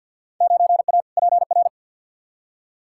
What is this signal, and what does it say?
Morse code, heard as a keyed tone, about 53 wpm, 700 Hz. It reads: 9A PR
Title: Field Day Exchanges Encore - 50wpm